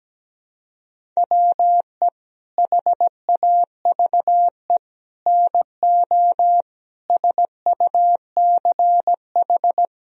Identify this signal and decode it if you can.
Morse code, heard as a keyed tone, about 17 wpm, 700 Hz. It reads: WE HAVE NO SUCH
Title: Sentences from Top 1000 Words - Review Part 1 - 17wpm